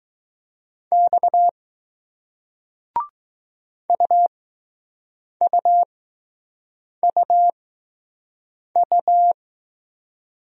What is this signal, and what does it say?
Morse code, heard as a keyed tone, about 20 wpm, spaced as for 6 wpm, 700 Hz. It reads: X UUUU